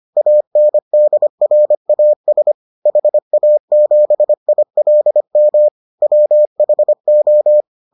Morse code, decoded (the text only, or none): ANDRAS HA7ILM WHO